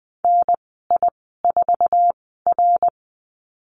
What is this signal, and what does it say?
Morse code, heard as a keyed tone, about 20 wpm, spaced as for 14 wpm, 700 Hz. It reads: NI4R
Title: 069 Lesson 12g - Letter d - Call Signs 20wpm